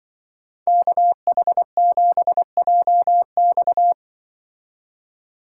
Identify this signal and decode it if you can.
Morse code, heard as a keyed tone, about 24 wpm, 700 Hz. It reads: KH7JX
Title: Speed Racing - USA 2x2 Call Signs 20wpm